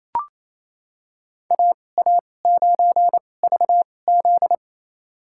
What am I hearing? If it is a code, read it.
Morse code, heard as a keyed tone, about 28 wpm, spaced as for 20 wpm, 700 Hz. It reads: AA9VZ
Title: USA Call Signs - 5 Characters - 2x Character Spacing - 28wpm